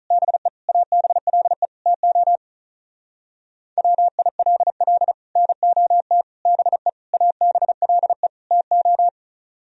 Morse code, heard as a keyed tone, about 34 wpm, 700 Hz. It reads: BE ABLE TO WILL NOT BE ABLE TO
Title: Speed-Racing - Sets of 5 Words 35wpm